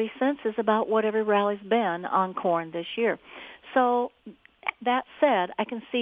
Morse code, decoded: none

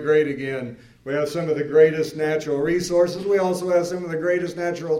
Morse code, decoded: none

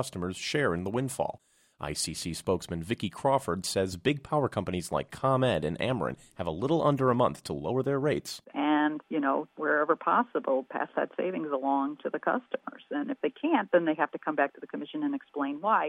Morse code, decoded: none

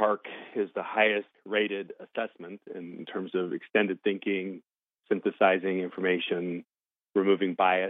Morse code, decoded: none

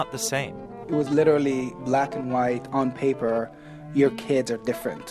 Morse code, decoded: none

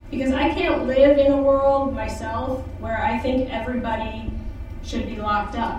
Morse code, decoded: none

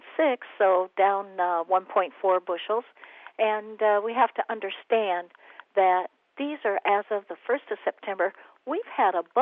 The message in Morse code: none

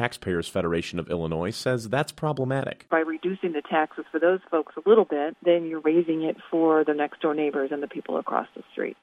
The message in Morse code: none

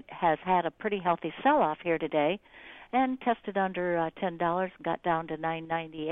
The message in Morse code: none

